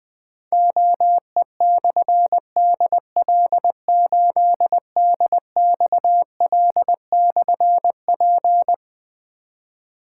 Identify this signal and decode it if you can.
Morse code, heard as a keyed tone, about 20 wpm, 700 Hz. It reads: OE/DL8DXL/P
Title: Difficult Call Signs 20wpm